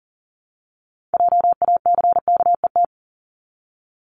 Morse code, heard as a keyed tone, about 39 wpm, 700 Hz. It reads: JACKET
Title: Multi-letter Phonemes Repeated 3x 40wpm